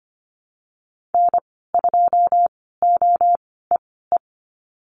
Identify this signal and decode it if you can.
Morse code, heard as a keyed tone, about 25 wpm, spaced as for 16 wpm, 700 Hz. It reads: N2OEE
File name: Lesson 15g - Number 5 - Call Signs 25wpm